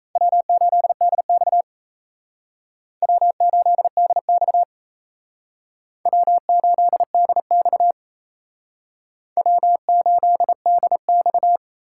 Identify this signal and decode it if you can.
Morse code, heard as a keyed tone, about 35 wpm, 700 Hz. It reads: W8DX W8DX W8DX W8DX
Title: Speed Racing - USA 1x2 Call Signs 28wpm